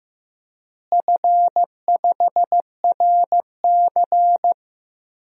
Morse code, heard as a keyed tone, about 15 wpm, 700 Hz. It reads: F5RC
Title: DXCC-100 Call Signs - 4 Characters - Repeated 2x - 15wpm